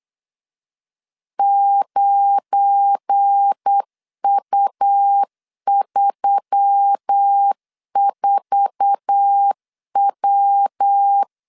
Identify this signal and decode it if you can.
Morse code, heard as a keyed tone, about 8 wpm, 800 Hz. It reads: 9U34W
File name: CW / DET / 1 9 1500C S B 8wpm 5624